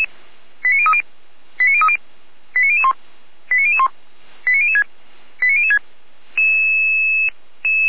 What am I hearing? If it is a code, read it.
Five-tone selective call (ZVEI), heard as quick runs of five tones, tones of about 70 ms. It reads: 89030 89030 89001 89001 89006 89006